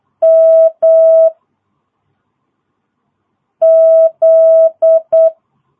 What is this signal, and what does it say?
Morse code, heard as a keyed tone, about 8 wpm, 650 Hz. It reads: M Z
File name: jh8cbh / 2022cw